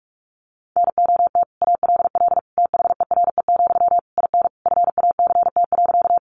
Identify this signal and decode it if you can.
Morse code, heard as a keyed tone, about 44 wpm, 700 Hz. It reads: NOT ALL THERE, IN FACT.
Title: Sentences from Top 600 Words 45wpm